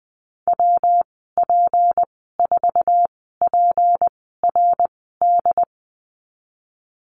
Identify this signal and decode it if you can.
Morse code, heard as a keyed tone, about 20 wpm, spaced as for 14 wpm, 700 Hz. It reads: WP4PRD
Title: 179 Lesson 29g - Slash - Call Signs 20wpm